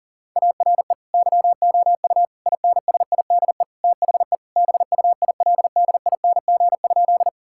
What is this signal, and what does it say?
Morse code, heard as a keyed tone, about 40 wpm, 700 Hz. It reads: ARE YOU INSIDE THE BUILDING?
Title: Sentences from Top 1000 Words - Review Part 6 - 40wpm